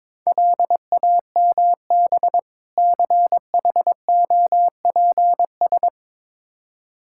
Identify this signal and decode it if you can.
Morse code, heard as a keyed tone, about 22 wpm, 700 Hz. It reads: LAMB CHOPS
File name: Difficult Words with Hints - 22wpm